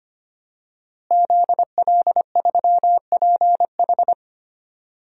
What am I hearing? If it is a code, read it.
Morse code, heard as a keyed tone, about 25 wpm, 700 Hz. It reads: ZL3PH
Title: DXCC-100 Call Signs - 5 Characters - Repeated 2x - 25wpm